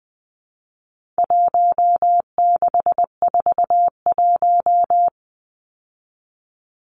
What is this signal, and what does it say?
Morse code, heard as a keyed tone, about 20 wpm, 700 Hz. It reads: 1641
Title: Numbers - Contest Style - 4-Digit 20wpm